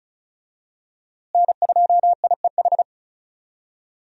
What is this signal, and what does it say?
Morse code, heard as a keyed tone, about 34 wpm, 700 Hz. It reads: N2IEH
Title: USA Call Signs - 5 Characters - Repeated 3x - 35wpm